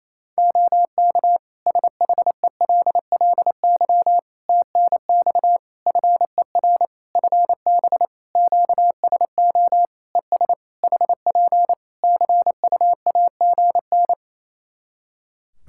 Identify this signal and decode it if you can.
Morse code, heard as a keyed tone, about 27 wpm, 700 Hz. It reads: OK SHELLY TNX FER FB QSO ES HP CUAGN
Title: Practice QSO - Exchange 4 - Ending - 28wpm